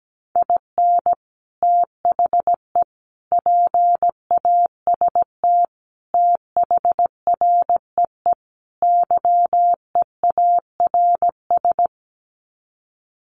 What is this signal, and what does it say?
Morse code, heard as a keyed tone, about 17 wpm, 700 Hz. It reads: IN THE PAST THREE YEARS
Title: Sets of 5 Words Encore2 17wpm